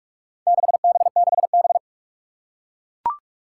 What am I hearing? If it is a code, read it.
Morse code, heard as a keyed tone, about 44 wpm, 700 Hz. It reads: 6B6B
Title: Four Challenging Letter-Number Combinations 45wpm